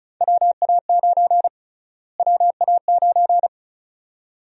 Morse code, heard as a keyed tone, about 35 wpm, 700 Hz. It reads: WA9 WA9
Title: USA Call Sign Prefixes - Repeated 2x - 35wpm